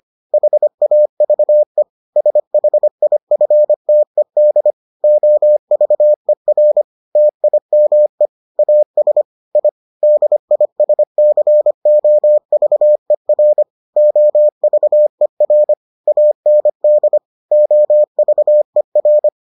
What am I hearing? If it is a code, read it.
Morse code, heard as a keyed tone, about 25 wpm, 600 Hz. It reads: HAVE SHIFTED OVER TIME AS I DISCOVER OVER AND OVER